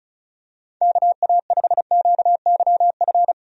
Morse code, heard as a keyed tone, about 34 wpm, 700 Hz. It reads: KA5QYF
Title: USA Call Signs - 6 Characters - 35wpm